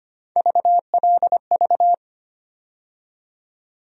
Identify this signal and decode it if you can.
Morse code, heard as a keyed tone, about 25 wpm, 700 Hz. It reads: VLV
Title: USA Call Sign Postfixes - 3 Letters 25wpm